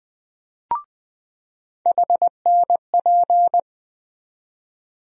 Morse code, heard as a keyed tone, about 20 wpm, 700 Hz. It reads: HNP
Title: USA Call Sign Postfixes - 3 Letters 20wpm